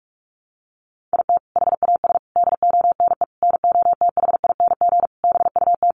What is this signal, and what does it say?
Morse code, heard as a keyed tone, about 44 wpm, 700 Hz. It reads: IT HAS DONE NOTHING BUT
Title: Sentences from Top 1100 Words 45wpm